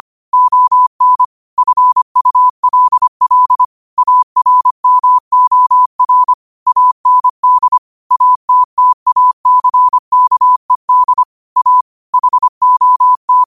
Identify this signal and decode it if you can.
Morse code, heard as a keyed tone, about 25 wpm, 1000 Hz. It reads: ON FULL ARMOR AND ATTACKED A HOT